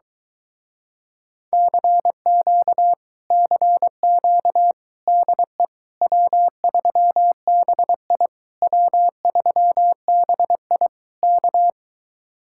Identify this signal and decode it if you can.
Morse code, heard as a keyed tone, about 23 wpm, 700 Hz. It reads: CQ CQ DE W3BI W3BI K